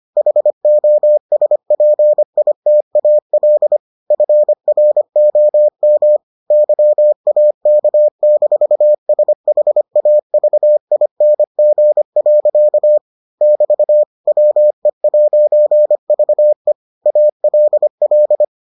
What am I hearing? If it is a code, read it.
Morse code, heard as a keyed tone, about 25 wpm, 600 Hz. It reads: HOSPITAL FROM YAK-SHAVING. <BT> WE'VE ALL